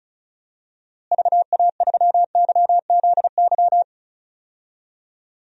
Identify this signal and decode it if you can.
Morse code, heard as a keyed tone, about 34 wpm, 700 Hz. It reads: VA3YZY
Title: DXCC-100 Call Signs - 6 Characters - Repeated 2x - 35wpm